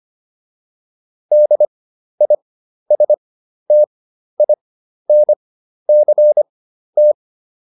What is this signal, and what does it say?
Morse code, heard as a keyed tone, about 25 wpm, spaced as for 12 wpm, 600 Hz. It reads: DISTINCT